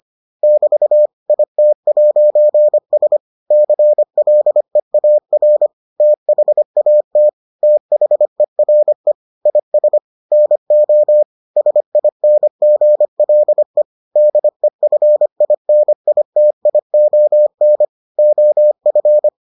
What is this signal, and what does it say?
Morse code, heard as a keyed tone, about 25 wpm, 600 Hz. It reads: <BT> IT'S CLEAR THAT THERE IS NO SINGLE DEFINITION OF